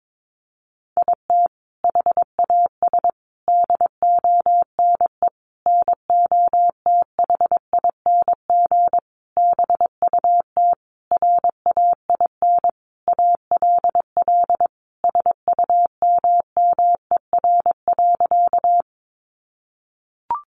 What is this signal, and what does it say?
Morse code, heard as a keyed tone, about 22 wpm, 700 Hz. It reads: IT HAS DONE NOTHING BUT RAIN ALL SUMMER.
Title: Sentences from Top 1100 Words 22wpm